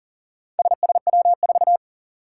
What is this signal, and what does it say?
Morse code, heard as a keyed tone, about 42 wpm, 700 Hz. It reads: SSW4
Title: Four Challenging Letter-Number Combinations 40wpm